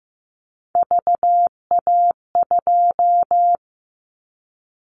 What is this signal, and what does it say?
Morse code, heard as a keyed tone, about 15 wpm, 700 Hz. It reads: VA2
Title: Callsign Prefixes for Top 100 DXCC - 15wpm